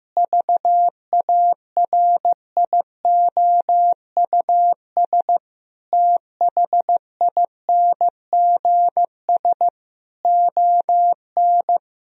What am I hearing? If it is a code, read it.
Morse code, heard as a keyed tone, about 15 wpm, 700 Hz. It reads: VARIOUS THINGS ON
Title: Sentences from Top 1000 Words - Review Part 4 - 15wpm